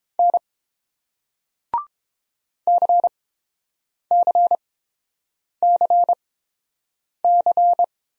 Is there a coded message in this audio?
Yes: Morse code from 0.2 to 7.8 s, heard as a keyed tone, about 30 wpm, spaced as for 7 wpm, 700 Hz. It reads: N CCCC